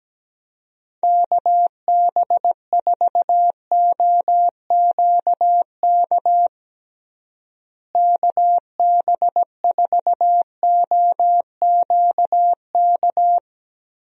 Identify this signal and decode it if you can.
Morse code, heard as a keyed tone, about 17 wpm, 700 Hz. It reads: KB4OQK KB4OQK
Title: USA Call Signs - 6 Characters - Repeated 3x - 17wpm